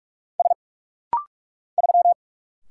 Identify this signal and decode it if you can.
Morse code, heard as a keyed tone, about 44 wpm, 700 Hz. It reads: S 3
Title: Single Letter-Number - Rapid-Fire 45wpm